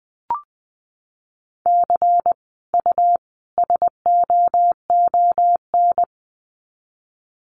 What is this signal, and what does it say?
Morse code, heard as a keyed tone, about 20 wpm, 700 Hz. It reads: C U SOON
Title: Common QSO Phrases 20wpm